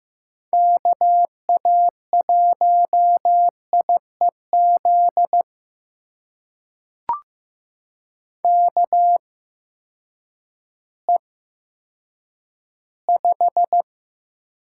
Morse code, heard as a keyed tone, about 15 wpm, spaced as for 9 wpm, 700 Hz. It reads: KA1IEZ KE5
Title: USA Call Signs - 6 Characters - 8x Character Spacing - 15wpm